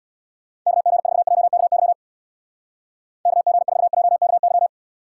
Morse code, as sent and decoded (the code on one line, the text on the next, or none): -.- --. ....- .--- -.- -.-- / -.- --. ....- .--- -.- -.--
KG4JKY KG4JKY